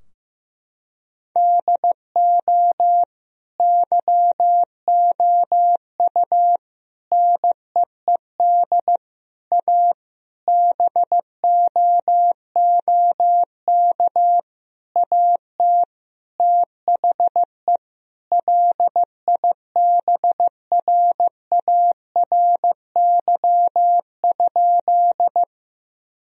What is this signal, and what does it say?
Morse code, heard as a keyed tone, about 16 wpm, 700 Hz. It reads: DO YOU NEED A BOOK AT THE LIBRARY?
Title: Sentences from Top 1000 Words - Review Part 4 - 15wpm